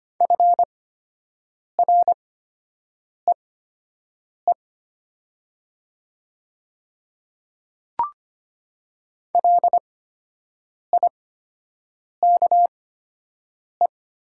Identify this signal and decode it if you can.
Morse code, heard as a keyed tone, about 25 wpm, spaced as for 7 wpm, 700 Hz. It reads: FREE LIKE
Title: 8x Character Spacing - Top 500 Words - 25wpm